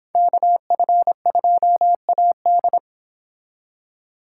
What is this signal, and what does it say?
Morse code, heard as a keyed tone, about 26 wpm, 700 Hz. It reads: KF2AD